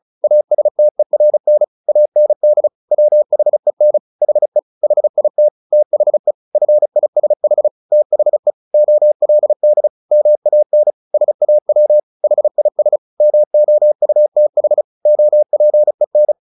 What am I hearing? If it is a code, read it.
Morse code, heard as a keyed tone, about 34 wpm, 600 Hz. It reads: ASTERN AND WHEN HE HIT THE FISH THE OLD MAN SAW HIS MOUTH OPEN
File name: the old man and the sea 2 35wpm